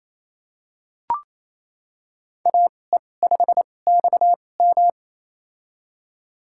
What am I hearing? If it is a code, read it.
Morse code, heard as a keyed tone, about 28 wpm, spaced as for 20 wpm, 700 Hz. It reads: AE5XM
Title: USA Call Signs - 2x Character Spacing - 28wpm